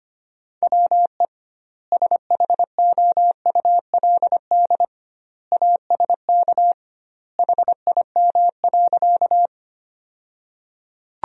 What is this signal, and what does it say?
Morse code, heard as a keyed tone, about 25 wpm, 700 Hz. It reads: WE SHOULD ASK HIM.
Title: 2x Word-Spacing - Sentences from Top 300 Words - 25wpm